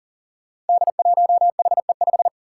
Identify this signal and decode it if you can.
Morse code, heard as a keyed tone, about 40 wpm, 700 Hz. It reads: D1HE5
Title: Five Challenging Letter-Number Combinations 40wpm